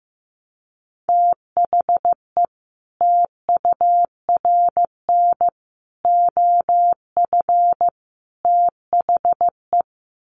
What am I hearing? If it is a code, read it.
Morse code, heard as a keyed tone, about 15 wpm, 700 Hz. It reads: THE TURN OF THE
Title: Sets of 4 Words 15wpm